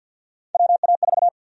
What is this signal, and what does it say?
Morse code, heard as a keyed tone, about 53 wpm, 700 Hz. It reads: WA4
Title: USA Call Sign Prefixes - 50wpm